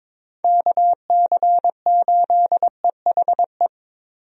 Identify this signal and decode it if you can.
Morse code, heard as a keyed tone, about 22 wpm, 700 Hz. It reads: KC8EHE
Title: USA Call Signs - 6 Characters - Repeated 3x - 22wpm